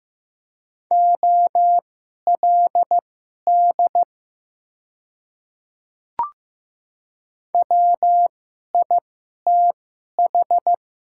Morse code, heard as a keyed tone, about 15 wpm, spaced as for 11 wpm, 700 Hz. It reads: OLD WITH
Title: 2x Character Spacing - Top 300 Words - 15wpm